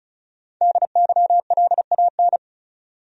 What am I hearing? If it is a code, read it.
Morse code, heard as a keyed tone, about 34 wpm, 700 Hz. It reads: DYLAN
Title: USA Common Names - 35wpm